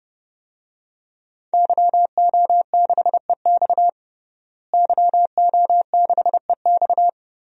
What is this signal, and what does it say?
Morse code, heard as a keyed tone, about 29 wpm, 700 Hz. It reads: YO6EX YO6EX
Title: DXCC-100 Call Signs - 5 Characters - Repeated 3x - 30wpm